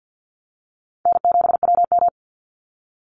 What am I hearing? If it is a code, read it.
Morse code, heard as a keyed tone, about 51 wpm, 700 Hz. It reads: N7WM
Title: NA SOTA Chaser Callsigns 50wpm